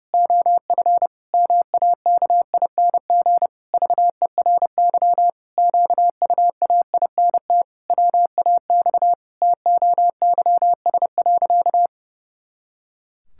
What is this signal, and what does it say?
Morse code, heard as a keyed tone, about 30 wpm, 700 Hz. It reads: OF MAKING VERY QUAINT WAX TOYS.
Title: Pangram Challenge v2 - 30wpm